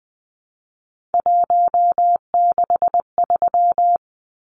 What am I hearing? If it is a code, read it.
Morse code, heard as a keyed tone, about 20 wpm, 700 Hz. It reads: 163
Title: Numbers - 3-Digit 20wpm